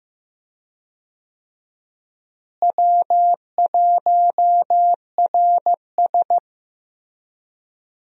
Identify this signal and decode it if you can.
Morse code, heard as a keyed tone, about 15 wpm, 700 Hz. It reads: W1RS
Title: USA Call Signs - 4 Characters - Repeated 3x - 15wpm